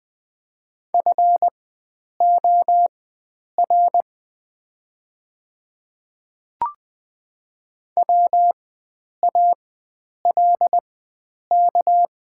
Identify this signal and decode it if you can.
Morse code, heard as a keyed tone, about 20 wpm, spaced as for 9 wpm, 700 Hz. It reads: FOR WALK
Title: 4x Character Spacing - Top 500 Words - 20wpm